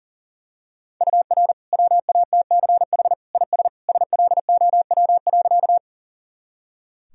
Morse code, heard as a keyed tone, about 40 wpm, 700 Hz. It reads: UR WATCH IS SLOW.
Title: Sentences from Top 400 Words and Abbreviations 40wpm